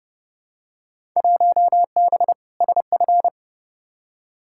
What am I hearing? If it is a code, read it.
Morse code, heard as a keyed tone, about 31 wpm, 700 Hz. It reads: 1B SF
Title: Field Day Exchanges Encore - 30wpm